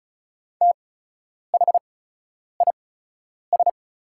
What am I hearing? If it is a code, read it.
Morse code, heard as a keyed tone, about 36 wpm, spaced as for 10 wpm, 700 Hz. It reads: THIS
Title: 8x Character Spacing - Top 200 Words - 35wpm